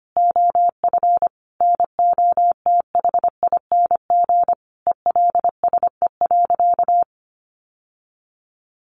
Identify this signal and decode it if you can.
Morse code, heard as a keyed tone, about 25 wpm, 700 Hz. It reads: OF NOTHING ELSE.
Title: Sentences from Top 1100 Words 25wpm